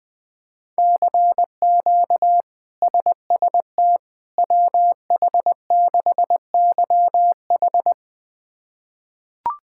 Morse code, heard as a keyed tone, about 20 wpm, 700 Hz. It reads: CQ SST WH6YH
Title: K1USN SST QSO Practice - 20wpm